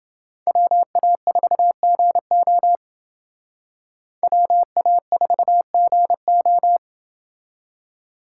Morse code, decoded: WA4GO WA4GO

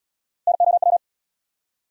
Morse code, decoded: NOW

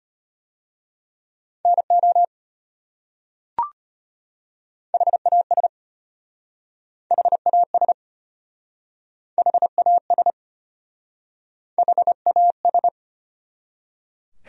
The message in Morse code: NO HAS HAS HAS HAS

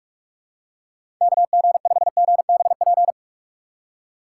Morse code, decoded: KG5GBP

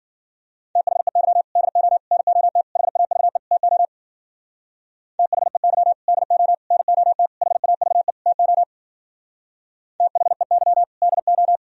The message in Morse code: THEY DO NOT HAVE TO THEY DO NOT HAVE TO THEY DO

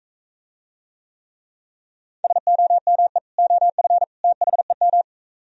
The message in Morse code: SOME OF THEM

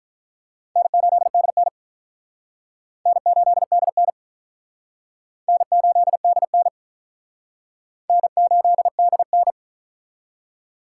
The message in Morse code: N8DN N8DN N8DN N8DN

